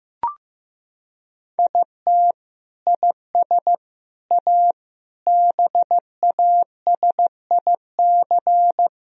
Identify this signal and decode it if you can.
Morse code, heard as a keyed tone, about 15 wpm, 700 Hz. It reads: IT IS A BASIC
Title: Sentences from Top 1000 Words - Review Part 3 - 15wpm